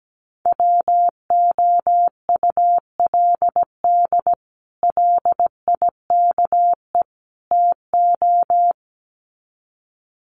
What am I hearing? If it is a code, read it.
Morse code, heard as a keyed tone, about 17 wpm, 700 Hz. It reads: WOULD LIKE TO
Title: Sets of 3 Words 17wpm